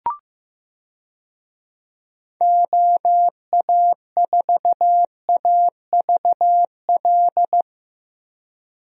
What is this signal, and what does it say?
Morse code, heard as a keyed tone, about 15 wpm, 700 Hz. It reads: OA4AVL
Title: USA Call Signs - 6 Characters - Repeated 3x - 15wpm